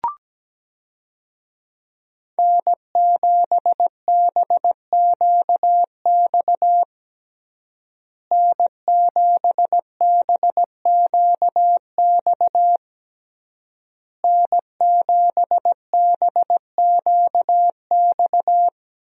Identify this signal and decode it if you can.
Morse code, heard as a keyed tone, about 17 wpm, 700 Hz. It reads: N7BQX N7BQX N7BQX
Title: USA Call Signs - 5 Characters - Repeated 3x - 17wpm